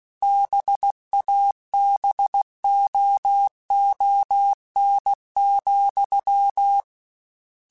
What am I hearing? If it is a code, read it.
Morse code, heard as a keyed tone, about 16 wpm, 800 Hz. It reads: BABOON,
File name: Misc / Misc-mp3s / BIG @15